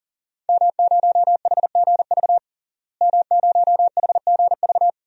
Morse code, heard as a keyed tone, about 40 wpm, 700 Hz. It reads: M0HGV M0HGV
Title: DXCC-100 Call Signs - 5 Characters - Repeated 3x - 40wpm